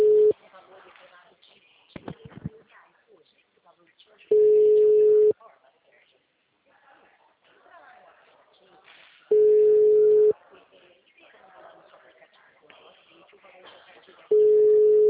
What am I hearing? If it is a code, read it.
Ringback tone (Europe, 425 Hz), with speech.
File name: out- -2019- - - .359